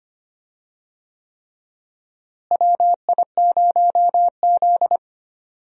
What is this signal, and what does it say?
Morse code, heard as a keyed tone, about 25 wpm, 700 Hz. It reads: WI0Z
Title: USA Call Signs - 4 Characters - Repeated 3x - 25wpm